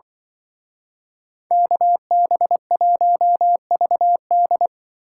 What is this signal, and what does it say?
Morse code, heard as a keyed tone, about 24 wpm, 700 Hz. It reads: KB1VD
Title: Speed Racing - USA 2x2 Call Signs 20wpm